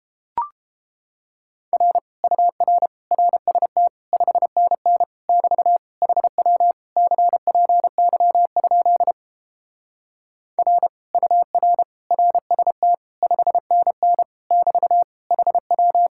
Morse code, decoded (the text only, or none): R UR RST 5NN <BT> HW CPY? R UR RST 5NN <BT> HW